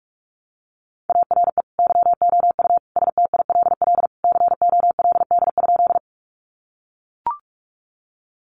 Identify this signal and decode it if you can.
Morse code, heard as a keyed tone, about 44 wpm, 700 Hz. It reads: ARE YOU STILL COLD?
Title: Sentences from Top 400 Words 45wpm